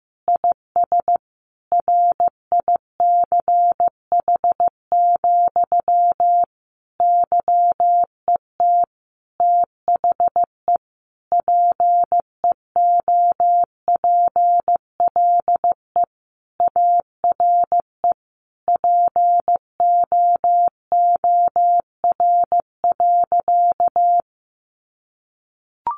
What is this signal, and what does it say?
Morse code, heard as a keyed tone, about 15 wpm, 700 Hz. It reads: IS RICH, YET THE PEOPLE ARE POOR.
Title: Sentences from Top 1100 Words 15wpm